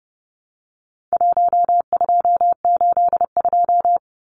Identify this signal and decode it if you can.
Morse code, heard as a keyed tone, about 30 wpm, 700 Hz. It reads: 1282
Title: Numbers - 4-Digit 30wpm